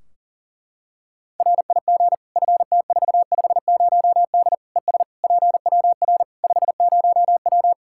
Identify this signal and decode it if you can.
Morse code, heard as a keyed tone, about 40 wpm, 700 Hz. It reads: RIG FT450D ES PWR 50W
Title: K1USN SST QSO Practice 40wpm